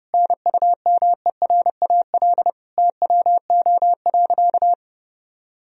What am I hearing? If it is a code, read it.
Morse code, heard as a keyed tone, about 30 wpm, 700 Hz. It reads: NUMERAL TWO.
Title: Sentences from Top 1000 Words - Review Part 2 - 30wpm